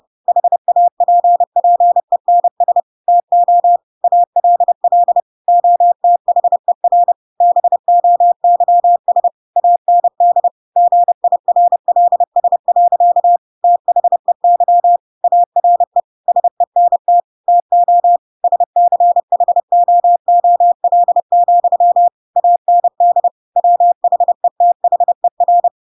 Morse code, heard as a keyed tone, about 30 wpm, 700 Hz. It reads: HAPPENS TO ALL OTHER BOYS AND GIRLS. THEY ARE SENT TO SCHOOL, AND WHETHER
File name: Pinocchio / Pinocchio ch4 30 30 ik1hge